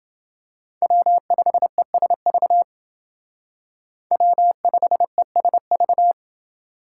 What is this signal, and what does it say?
Morse code, heard as a keyed tone, about 28 wpm, 700 Hz. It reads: W5ESV W5ESV